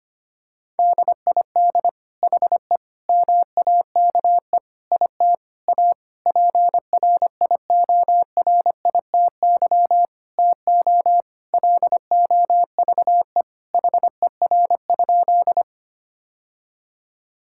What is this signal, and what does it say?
Morse code, heard as a keyed tone, about 25 wpm, 700 Hz. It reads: DID HE MAKE IT A PRIORITY TO LOVE HER?
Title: Sentences from Top 1600 Words 25wpm